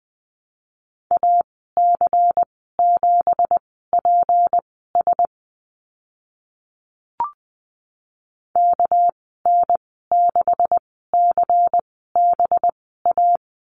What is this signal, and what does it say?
Morse code, heard as a keyed tone, about 19 wpm, spaced as for 14 wpm, 700 Hz. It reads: AC7PS KN6CBA
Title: 200 Lesson 32g - Letter k - Call Signs 20wpm